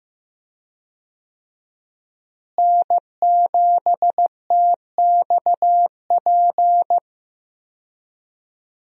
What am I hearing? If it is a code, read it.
Morse code, heard as a keyed tone, about 15 wpm, 700 Hz. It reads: N7TXP
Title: USA Call Signs - 5 Characters - Repeated 2x - 15wpm